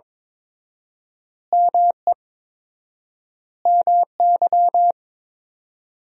Morse code, heard as a keyed tone, about 21 wpm, 700 Hz. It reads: ME MY
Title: 4x Word-Spacing - Sentences from Top 200 Words - 22wpm